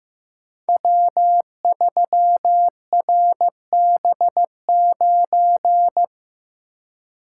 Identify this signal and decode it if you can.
Morse code, heard as a keyed tone, about 15 wpm, 700 Hz. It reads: W3RB9
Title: Code Groups - 5 Letter:Numbers - 15wpm